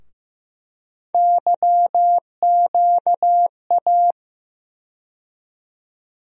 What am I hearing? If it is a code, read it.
Morse code, heard as a keyed tone, about 15 wpm, 700 Hz. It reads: YQA
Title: USA Call Sign Postfixes - 3 Letters 15wpm